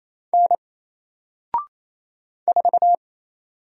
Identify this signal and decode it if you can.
Morse code, heard as a keyed tone, about 27 wpm, 700 Hz. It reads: N 4
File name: Single-Letter-Number-28wpm